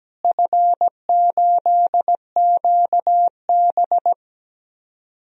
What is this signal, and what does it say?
Morse code, heard as a keyed tone, about 17 wpm, 700 Hz. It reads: F8QB
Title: DXCC-100 Call Signs - 4 Characters - Repeated 3x - 17wpm